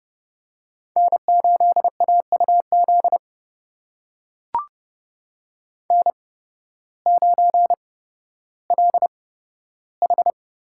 Morse code, heard as a keyed tone, about 31 wpm, spaced as for 14 wpm, 700 Hz. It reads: N8AUZ N9LH
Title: USA Call Signs - 5 Characters - 8x Character Spacing - 30wpm